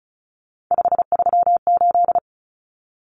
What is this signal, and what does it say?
Morse code, heard as a keyed tone, about 34 wpm, 700 Hz. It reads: 538
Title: Numbers - 3-Digit 35wpm